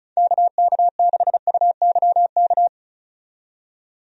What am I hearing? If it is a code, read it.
Morse code, heard as a keyed tone, about 36 wpm, 700 Hz. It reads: KK6UYK
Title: USA Call Signs - 6 Characters - 35wpm